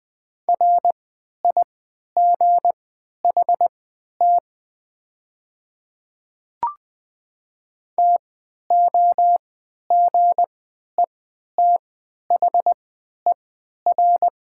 Morse code, heard as a keyed tone, about 20 wpm, spaced as for 11 wpm, 700 Hz. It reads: RIGHT TOGETHER
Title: 3x Character Spacing - Top 300 Words - 20wpm